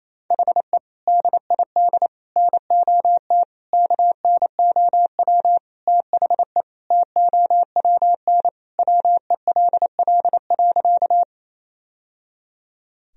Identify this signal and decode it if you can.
Morse code, heard as a keyed tone, about 28 wpm, 700 Hz. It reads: HE DID NOT KNOW THE TOWN WELL.
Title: Sentences from Top 1000 Words - Review Part 2 - 28wpm